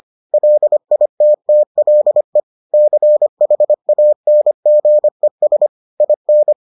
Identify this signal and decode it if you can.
Morse code, heard as a keyed tone, about 25 wpm, 600 Hz. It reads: LITTLE CHANGES IN